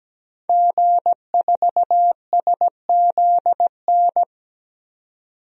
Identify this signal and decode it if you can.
Morse code, heard as a keyed tone, about 17 wpm, 700 Hz. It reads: G4SZN
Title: DXCC-100 Call Signs - 5 Characters - 17wpm